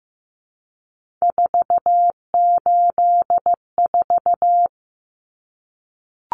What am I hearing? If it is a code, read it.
Morse code, heard as a keyed tone, about 15 wpm, 700 Hz. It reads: 484
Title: Numbers - 3-Digit 15wpm